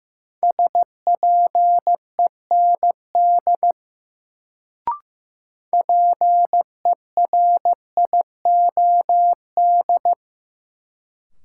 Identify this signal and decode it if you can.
Morse code, heard as a keyed tone, about 15 wpm, 700 Hz. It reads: SPEND PERIOD
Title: Sets of 5 Words - New Word Drill - 15wpm